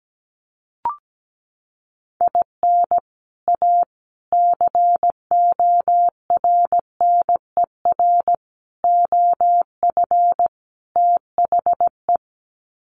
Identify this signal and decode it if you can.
Morse code, heard as a keyed tone, about 17 wpm, 700 Hz. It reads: IN A CORNER OF THE
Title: Sets of 5 Words Encore2 17wpm